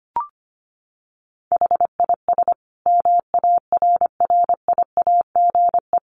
Morse code, heard as a keyed tone, about 25 wpm, 700 Hz. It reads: HIS MARRIAGE
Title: Sentences from Top 1500 Words 25wpm